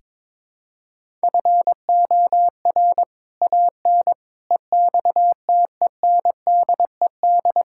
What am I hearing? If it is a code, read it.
Morse code, heard as a keyed tone, about 22 wpm, 700 Hz. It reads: FOR AN EXTENDED
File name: practice / 22wpm / Sets of 6 Words - 22wpm